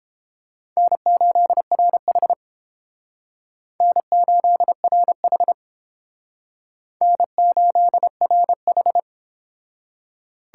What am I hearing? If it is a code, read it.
Morse code, heard as a keyed tone, about 31 wpm, 700 Hz. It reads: N8RH N8RH N8RH